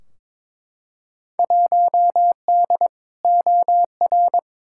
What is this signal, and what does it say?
Morse code, heard as a keyed tone, about 22 wpm, 700 Hz. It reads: 1D OR
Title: Field Day Exchanges - 22wpm